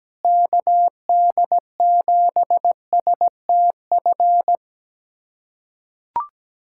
Morse code, heard as a keyed tone, about 17 wpm, 700 Hz. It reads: KD7STF
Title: USA Callsign DE Callsign 17wpm